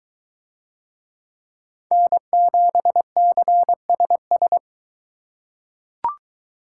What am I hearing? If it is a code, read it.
Morse code, heard as a keyed tone, about 22 wpm, 700 Hz. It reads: N7CSS